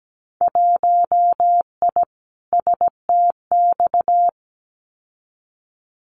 Morse code, heard as a keyed tone, about 17 wpm, 700 Hz. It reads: 1I STX
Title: Winter Field Day Exchanges 17wpm